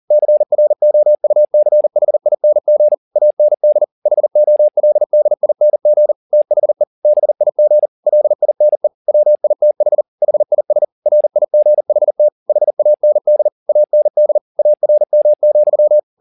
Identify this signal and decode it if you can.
Morse code, heard as a keyed tone, about 40 wpm, 600 Hz. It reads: CROUCHING AND HOLDING THE BIG LINE WITH HIS RIGHT HAND AND ARM,
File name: the old man and the sea 2 40wpm